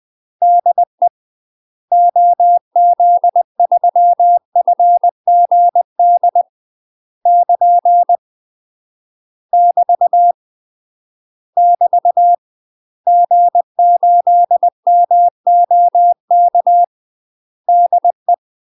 Morse code, heard as a keyed tone, about 20 wpm, 700 Hz. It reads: DE OZ3FGD <KN> <BT> <BT> G8MOK DE
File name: UK MASTER QSOs TRAINING/20WPM